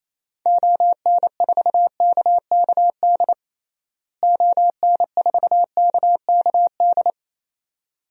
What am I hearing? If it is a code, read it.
Morse code, heard as a keyed tone, about 28 wpm, 700 Hz. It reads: ON4KKD ON4KKD
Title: DXCC-100 Call Signs - 6 Characters - Repeated 2x - 28wpm